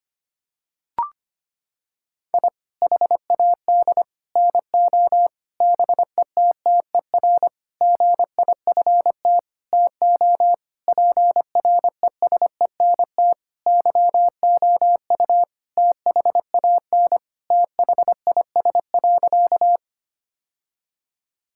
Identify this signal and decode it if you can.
Morse code, heard as a keyed tone, about 25 wpm, 700 Hz. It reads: I HAD NO BETTER GIFT TO PRESENT YOU THAN THIS.
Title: Sentences from Top 1000 Words - Review Part 6 - 25wpm